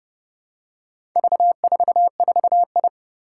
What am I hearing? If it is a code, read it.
Morse code, heard as a keyed tone, about 30 wpm, 700 Hz. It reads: V44I
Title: Four Challenging Letter-Number Combinations 30wpm